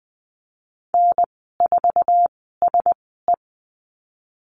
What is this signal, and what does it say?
Morse code, heard as a keyed tone, about 20 wpm, spaced as for 15 wpm, 700 Hz. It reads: N4SE
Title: 048 Lesson 9g - Number 4 - Call Signs 20wpm